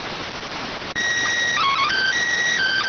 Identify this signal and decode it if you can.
SSTV (slow-scan television), PD 290.